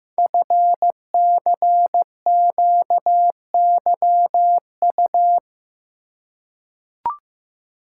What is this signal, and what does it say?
Morse code, heard as a keyed tone, about 15 wpm, 700 Hz. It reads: FCQYU
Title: Code Groups - 5 Letters - 15wpm